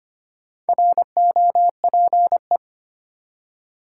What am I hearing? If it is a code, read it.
Morse code, heard as a keyed tone, about 25 wpm, 700 Hz. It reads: ROPE
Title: Four Letter Words - 25wpm